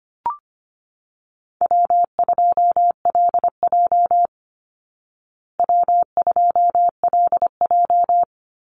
Morse code, heard as a keyed tone, about 25 wpm, 700 Hz. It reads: W2LJ W2LJ
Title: Easy Call Signs Repeated 2x - 25wpm